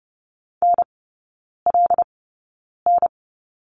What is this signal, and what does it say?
Morse code, heard as a keyed tone, about 30 wpm, spaced as for 9 wpm, 700 Hz. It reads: NLN
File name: course / 30:10wpm / 087 Lesson 15e - Number 5 - Three Characters 30wpm